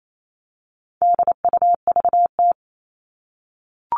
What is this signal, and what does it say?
Morse code, heard as a keyed tone, about 28 wpm, 700 Hz. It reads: DUVT
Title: All Cut Numbers - 4-Digit 28wpm